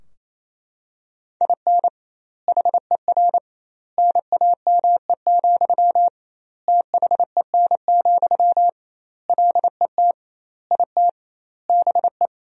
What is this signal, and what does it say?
Morse code, heard as a keyed tone, about 28 wpm, 700 Hz. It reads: IN HER NAME, THEN, LET IT BE
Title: 2x Word-Spacing - Sentences from Top 500 Words - 28wpm